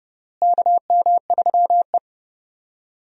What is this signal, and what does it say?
Morse code, heard as a keyed tone, about 30 wpm, 700 Hz. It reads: KM3E
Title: USA Call Signs - 4 Characters - Repeated 3x - 30wpm